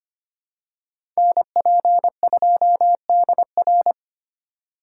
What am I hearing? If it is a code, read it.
Morse code, heard as a keyed tone, about 25 wpm, 700 Hz. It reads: NP2DR
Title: USA Call Signs - 5 Characters - 4x Character Spacing - 25wpm